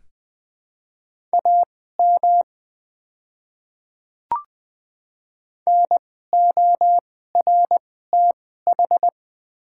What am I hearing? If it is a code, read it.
Morse code, heard as a keyed tone, about 20 wpm, spaced as for 15 wpm, 700 Hz. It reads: AM NORTH
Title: 2x Character Spacing - Top 500 Words - 20wpm